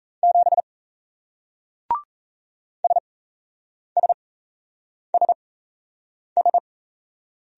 Morse code, heard as a keyed tone, about 38 wpm, spaced as for 8 wpm, 700 Hz. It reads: 7 SSSS